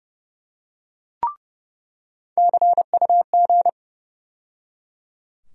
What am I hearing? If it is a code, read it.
Morse code, heard as a keyed tone, about 30 wpm, 700 Hz. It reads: CUG
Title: USA Call Sign Postfixes - 3 Letters 30wpm